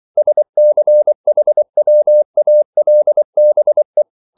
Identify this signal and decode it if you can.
Morse code, heard as a keyed tone, about 24 wpm, 600 Hz. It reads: SCHWALBE
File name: Liebespaar2 24wpm